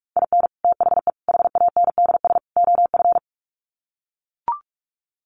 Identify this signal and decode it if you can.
Morse code, heard as a keyed tone, about 44 wpm, 700 Hz. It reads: IN THE HANDS OF
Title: Sets of 4 Words 45wpm